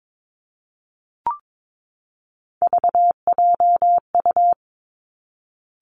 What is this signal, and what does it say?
Morse code, heard as a keyed tone, about 22 wpm, 700 Hz. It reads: VJU
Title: Three Challenging Letter-Number Combinations 22wpm